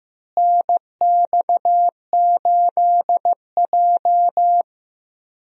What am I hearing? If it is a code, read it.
Morse code, heard as a keyed tone, about 15 wpm, 700 Hz. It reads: NX8J
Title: USA Call Signs - 4 Characters - Repeated 3x - 15wpm